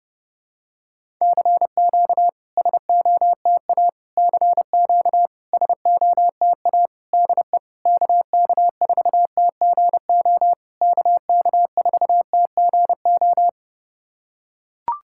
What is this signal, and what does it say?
Morse code, heard as a keyed tone, about 30 wpm, 690 Hz. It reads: CQ SOTA CQ SOTA DE KK4TGO KK4TGO
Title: SOTA QSO Practice 30wpm